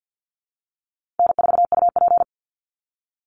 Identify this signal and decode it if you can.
Morse code, heard as a keyed tone, about 49 wpm, 700 Hz. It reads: N4UP
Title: Easy Call Signs Repeated 3x - 50wpm